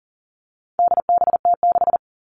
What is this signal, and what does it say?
Morse code, heard as a keyed tone, about 40 wpm, 700 Hz. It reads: DBT6
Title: All Cut Numbers - 4-Digit 40wpm